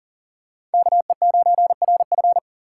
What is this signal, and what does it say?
Morse code, heard as a keyed tone, about 39 wpm, 700 Hz. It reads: KE9RF
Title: USA Call Signs - 3x Character Spacing - 40wpm